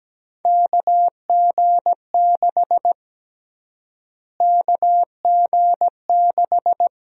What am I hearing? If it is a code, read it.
Morse code, heard as a keyed tone, about 17 wpm, 700 Hz. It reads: KG6 KG6
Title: USA Call Sign Prefixes - Repeated 2x - 17wpm